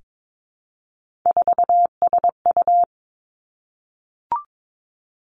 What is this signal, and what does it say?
Morse code, heard as a keyed tone, about 22 wpm, 700 Hz. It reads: 4SU
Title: Three Challenging Letter-Number Combinations 22wpm